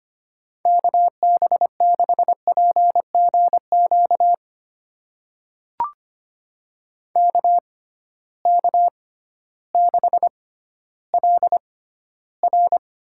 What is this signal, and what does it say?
Morse code, heard as a keyed tone, about 25 wpm, spaced as for 14 wpm, 700 Hz. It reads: KB6PGQ KK6LR
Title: USA Call Signs - 6 Characters - 6x Character Spacing - 25wpm